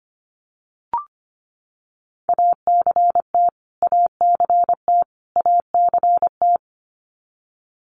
Morse code, heard as a keyed tone, about 25 wpm, 700 Hz. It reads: ACT ACT ACT
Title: Top 200 Words Repeated 25wpm